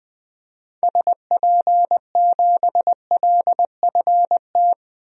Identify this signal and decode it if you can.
Morse code, heard as a keyed tone, about 20 wpm, 700 Hz. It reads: SP7LFT
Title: DXCC-100 Call Signs - 6 Characters - 20wpm